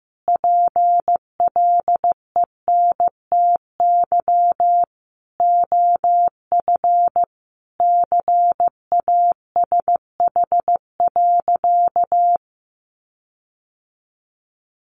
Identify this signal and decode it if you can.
Morse code, heard as a keyed tone, about 15 wpm, 700 Hz. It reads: PLENTY OF CASH.
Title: Sentences from Top 1300 Words 15wpm